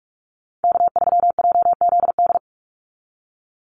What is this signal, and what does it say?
Morse code, heard as a keyed tone, about 46 wpm, 700 Hz. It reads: K3JZD
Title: NA SOTA Chaser Callsigns 45wpm